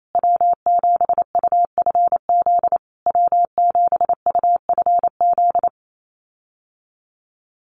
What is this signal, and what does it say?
Morse code, heard as a keyed tone, about 28 wpm, 700 Hz. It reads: W7UFZ W7UFZ
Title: USA Call Signs Repeated 3x 28wpm